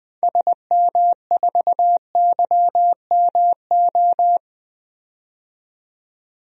Morse code, heard as a keyed tone, about 20 wpm, 700 Hz. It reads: SM4YMO
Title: DXCC-100 Call Signs - 6 Characters - 20wpm